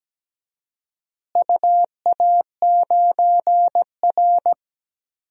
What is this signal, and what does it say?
Morse code, heard as a keyed tone, about 17 wpm, 700 Hz. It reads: UA9R
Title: DXCC-100 Call Signs - 4 Characters - Repeated 3x - 17wpm